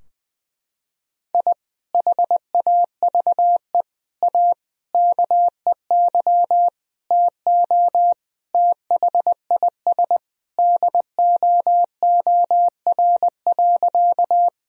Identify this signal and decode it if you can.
Morse code, heard as a keyed tone, about 20 wpm, 700 Hz. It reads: I HAVE A KEY TO THIS DOOR.
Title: Sentences from Top 1000 Words - Review Part 4 - 20wpm